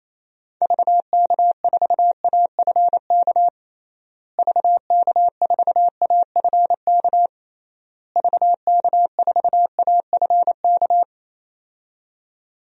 Morse code, heard as a keyed tone, about 28 wpm, 700 Hz. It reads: VK4AFK VK4AFK VK4AFK
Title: DXCC-100 Call Signs - 6 Characters - Repeated 3x - 28wpm